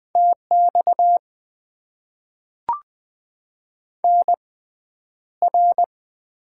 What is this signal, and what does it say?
Morse code, heard as a keyed tone, about 20 wpm, spaced as for 10 wpm, 700 Hz. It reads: TX NR
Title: Top 90 QSO Elements - 6x Character Spacing - 20wpm